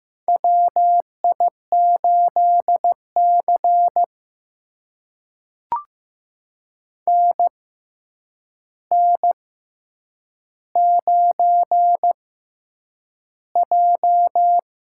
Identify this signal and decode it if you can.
Morse code, heard as a keyed tone, about 15 wpm, spaced as for 8 wpm, 700 Hz. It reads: WI8C NN9J
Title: USA Call Signs - 4 Characters - 6x Character Spacing - 15wpm